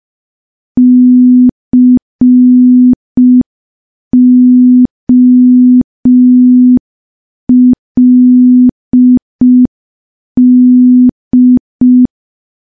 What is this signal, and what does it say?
Morse code, heard as a keyed tone, about 5 wpm, 250 Hz. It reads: COLD
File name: morse / morse-training-05WPM